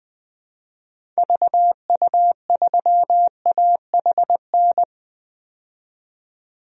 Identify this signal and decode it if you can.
Morse code, heard as a keyed tone, about 21 wpm, 700 Hz. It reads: VU3AHN
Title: DXCC-100 Call Signs - 6 Characters - 20wpm